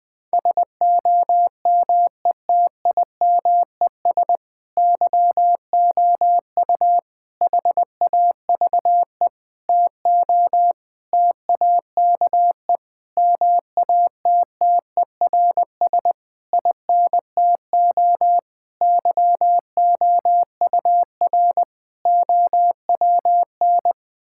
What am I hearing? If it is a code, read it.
Morse code, heard as a keyed tone, about 20 wpm, 700 Hz. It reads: SOMETIMES YOU HAVE TO TAKE MATTERS INTO YOUR OWN
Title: Sentences with 7-Word Phrases - Spoken Beginning - 20wpm